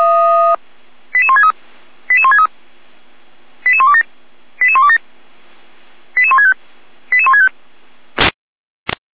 Five-tone selective call (ZVEI), heard as quick runs of five tones, tones of about 70 ms. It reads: 88163 88163 88127 88127 88165 88165
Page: /455/SDR/ZVEI/